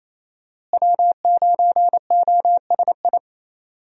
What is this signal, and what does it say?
Morse code, heard as a keyed tone, about 28 wpm, 700 Hz. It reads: W9OSI
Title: USA Call Signs - 5 Characters - Repeated 3x - 28wpm